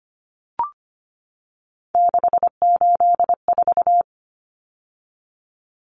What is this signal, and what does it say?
Morse code, heard as a keyed tone, about 25 wpm, 700 Hz. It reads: 684
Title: Numbers - 3-Digit 25wpm